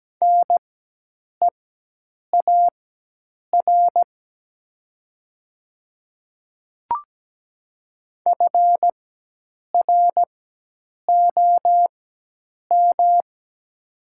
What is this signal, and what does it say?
Morse code, heard as a keyed tone, about 17 wpm, spaced as for 8 wpm, 700 Hz. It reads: NEAR FROM
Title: 4x Character Spacing - Top 500 Words - 17wpm